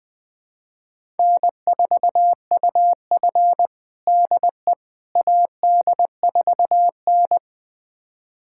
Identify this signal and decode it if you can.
Morse code, heard as a keyed tone, about 20 wpm, 700 Hz. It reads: N4UF DE AD4N
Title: USA Callsign DE Callsign - 4 Characters - 20wpm